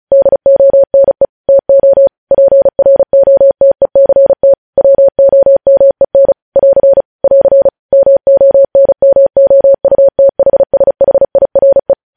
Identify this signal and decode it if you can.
Morse code, heard as a keyed tone, about 36 wpm, 550 Hz. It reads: DONE TO PROTECT WOMEN <AR> <AR> MONMOUTHSHIRE
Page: News headlines in Morse code at 35 wpm